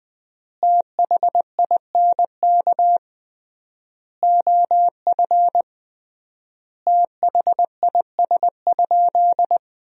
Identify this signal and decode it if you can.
Morse code, heard as a keyed tone, about 20 wpm, 700 Hz. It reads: THINK OF THIS?
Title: 3x Word-Spacing - Sentences from Top 200 Words - 20wpm